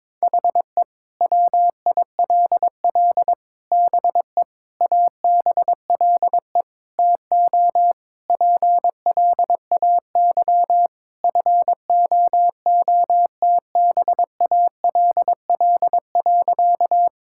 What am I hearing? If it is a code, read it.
Morse code, heard as a keyed tone, about 22 wpm, 700 Hz. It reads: HE WILL BE ABLE TO PLAY FOOTBALL.
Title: Sentences from Top 1600 Words 22wpm